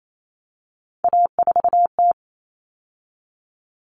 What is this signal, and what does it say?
Morse code, heard as a keyed tone, about 28 wpm, 700 Hz. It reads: A4T
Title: All Cut Numbers - 3-Digit 28wpm